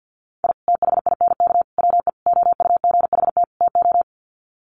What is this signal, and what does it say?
Morse code, heard as a keyed tone, about 50 wpm, 700 Hz. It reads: I THINK WE OUGHT TO
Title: Sets of 5 Words Encore 50wpm